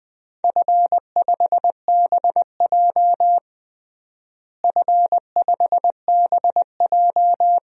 Morse code, heard as a keyed tone, about 20 wpm, 700 Hz. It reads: F5BJ F5BJ
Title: DXCC-100 Call Signs - 4 Characters - Repeated 2x - 20wpm